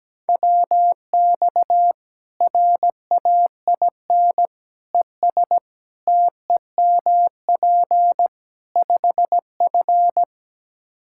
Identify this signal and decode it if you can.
Morse code, heard as a keyed tone, about 17 wpm, 700 Hz. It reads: WX RAIN ES TEMP 5F
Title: K1USN SST QSO Practice 17wpm